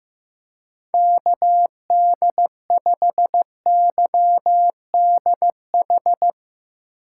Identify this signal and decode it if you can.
Morse code, heard as a keyed tone, about 15 wpm, 700 Hz. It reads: KD5YDH
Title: USA Call Signs - 6 Characters - 2x Character Spacing - 15wpm